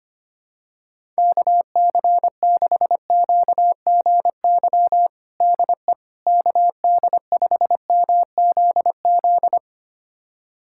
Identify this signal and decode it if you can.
Morse code, heard as a keyed tone, about 25 wpm, 700 Hz. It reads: KC6QGY DE KD5MZZ
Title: USA Callsign DE Callsign - 6 Characters - 25wpm